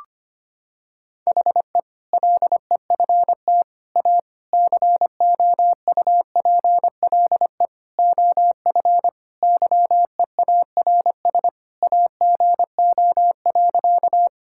Morse code, heard as a keyed tone, about 25 wpm, 700 Hz. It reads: HE LEFT A COUPLE OF YEARS AGO.
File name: practice / 25wpm / Sentences with 4-Word Phrases - 25wpm